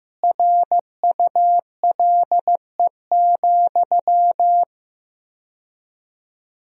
Morse code, heard as a keyed tone, about 15 wpm, 700 Hz. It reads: RULE,
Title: 4x Word-Spacing - Sentences from Top 400 Words - 15wpm